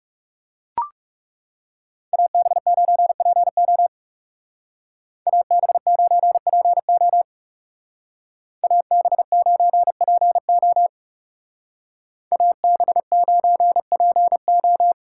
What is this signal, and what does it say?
Morse code, heard as a keyed tone, about 36 wpm, 700 Hz. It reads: AB9PO AB9PO AB9PO AB9PO
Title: Speed Racing - USA 2x2 Call Signs 30wpm